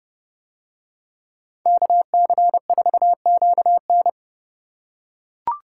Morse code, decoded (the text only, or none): KC4QN